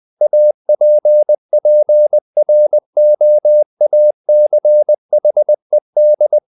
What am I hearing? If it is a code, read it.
Morse code, heard as a keyed tone, about 20 wpm, 600 Hz. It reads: APPROACHED
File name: the old man and the sea / the old man and the sea part 1 20wpm